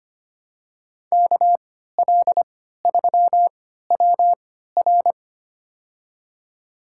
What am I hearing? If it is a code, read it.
Morse code, heard as a keyed tone, about 25 wpm, spaced as for 14 wpm, 700 Hz. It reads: KL3WR
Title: USA Call Signs - 5 Characters - 3x Character Spacing - 25wpm